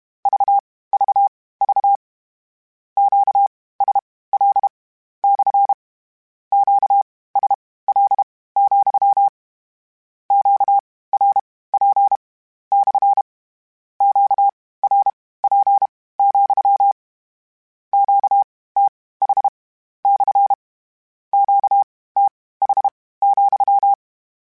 Morse code, heard as a keyed tone, about 32 wpm, spaced as for 18 wpm, 800 Hz. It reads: VVV QSL / QSL, QRP / QRP, QTH / QTH,
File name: Q Calls / Q Calls-mp3s / Q Calls SPSL2 @30